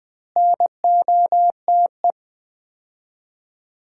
Morse code, heard as a keyed tone, about 20 wpm, 700 Hz. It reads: NOTE
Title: 3x Word-Spacing - Sets of 5 Words - 20wpm